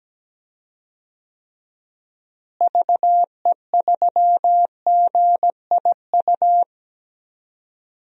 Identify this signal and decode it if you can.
Morse code, heard as a keyed tone, about 17 wpm, 700 Hz. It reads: VE3GIU
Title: DXCC-100 Call Signs - 6 Characters - Repeated 3x - 17wpm